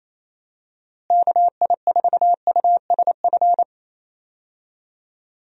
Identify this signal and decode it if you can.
Morse code, heard as a keyed tone, about 28 wpm, 700 Hz. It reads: KI4USF
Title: USA Call Signs - 6 Characters - Repeated 2x - 28wpm